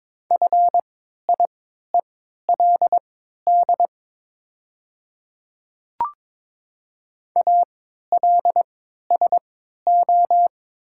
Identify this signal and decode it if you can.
Morse code, heard as a keyed tone, about 22 wpm, spaced as for 12 wpm, 700 Hz. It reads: FIELD ALSO
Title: 3x Character Spacing - Top 500 Words - 22wpm